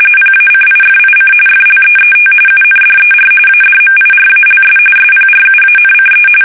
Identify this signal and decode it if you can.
RTTY, about 45.45 baud. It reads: PACIFIC FLEET UNDER ADMIRALS SPRUANCE
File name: fsk850 / battleshipnj